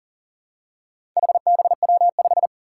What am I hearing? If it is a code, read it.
Morse code, heard as a keyed tone, about 40 wpm, 700 Hz. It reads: HBW5
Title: Four Challenging Letter-Number Combinations 40wpm